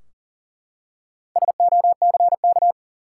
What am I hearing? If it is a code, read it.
Morse code, heard as a keyed tone, about 40 wpm, 700 Hz. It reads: SOCK
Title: Four Letter Words - 40wpm